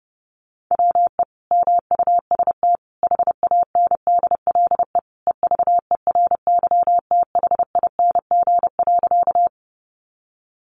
Morse code, heard as a keyed tone, about 30 wpm, 700 Hz. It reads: WE MUST HANDLE EVERYTHING.
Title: Sentences from Top 1100 Words 30wpm